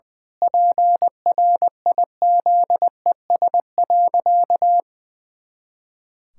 Morse code, heard as a keyed tone, about 20 wpm, 700 Hz. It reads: PRIZES.
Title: Pangram Challenge v2 - 20wpm